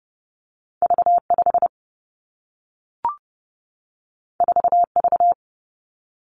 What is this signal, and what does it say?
Morse code, heard as a keyed tone, about 30 wpm, 700 Hz. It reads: V 5 4 V
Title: Two Challenging Letter-Number Combinations 30wpm